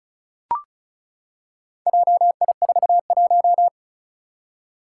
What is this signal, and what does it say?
Morse code, heard as a keyed tone, about 36 wpm, 700 Hz. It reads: JI41
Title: Four Challenging Letter-Number Combinations 35wpm